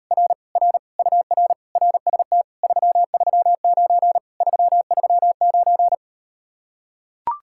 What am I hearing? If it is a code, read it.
Morse code, heard as a keyed tone, about 38 wpm, 700 Hz. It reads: R R UR RST 339 339